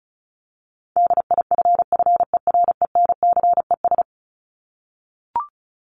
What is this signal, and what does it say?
Morse code, heard as a keyed tone, about 34 wpm, 700 Hz. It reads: DIFFERENCES
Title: ES:S Suffix - Top 100 Words with Base Words 35wpm